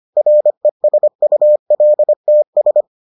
Morse code, heard as a keyed tone, about 25 wpm, 600 Hz. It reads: RESULTS